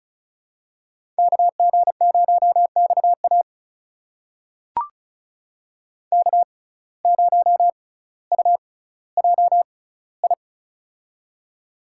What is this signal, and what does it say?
Morse code, heard as a keyed tone, about 34 wpm, spaced as for 18 wpm, 700 Hz. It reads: KG0XA K0UJI
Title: USA Call Signs - 5 Characters - 6x Character Spacing - 35wpm